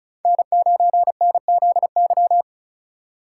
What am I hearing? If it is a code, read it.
Morse code, heard as a keyed tone, about 36 wpm, 700 Hz. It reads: N9NZY
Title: USA Call Signs - 5 Characters - 6x Character Spacing - 35wpm